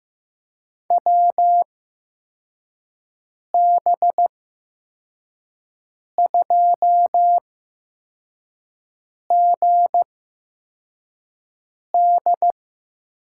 Morse code, heard as a keyed tone, about 15 wpm, spaced as for 4 wpm, 700 Hz. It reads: WB2GD
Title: USA Call Signs - 6 Characters - 8x Character Spacing - 15wpm